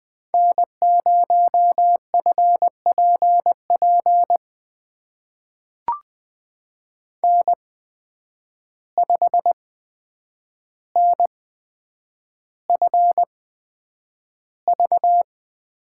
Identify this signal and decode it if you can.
Morse code, heard as a keyed tone, about 20 wpm, spaced as for 9 wpm, 700 Hz. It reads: N0FPP N5NFV
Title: USA Call Signs - 5 Characters - 8x Character Spacing - 20wpm